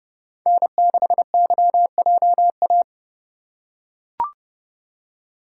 Morse code, heard as a keyed tone, about 30 wpm, 700 Hz. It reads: N6YJA